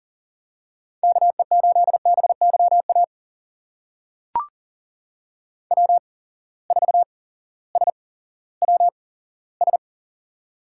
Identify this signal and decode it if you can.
Morse code, heard as a keyed tone, about 42 wpm, spaced as for 19 wpm, 700 Hz. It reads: KE8BYA W4SWS
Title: USA Call Signs - 8x Character Spacing - 40wpm